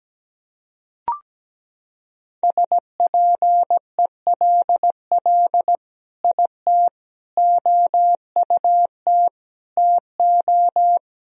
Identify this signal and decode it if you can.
Morse code, heard as a keyed tone, about 17 wpm, 700 Hz. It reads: SPELL IT OUT TO
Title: Sentences from Top 1000 Words - Review Part 1 - 17wpm